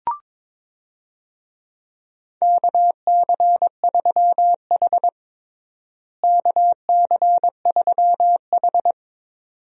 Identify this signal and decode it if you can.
Morse code, heard as a keyed tone, about 21 wpm, 700 Hz. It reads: KC3H KC3H
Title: USA Call Signs - 4 Characters - Repeated 2x - 22wpm